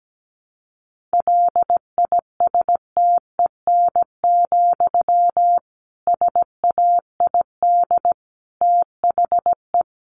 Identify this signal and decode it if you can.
Morse code, heard as a keyed tone, about 17 wpm, 700 Hz. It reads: LISTEN, SAID THE
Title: Sentences from Top 600 Words 17wpm